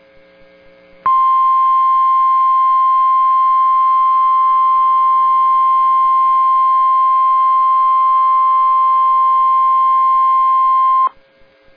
Emergency Alert System (SAME) tones (the 1050 Hz attention tone). It is heard from 1.0 to 11.1 s.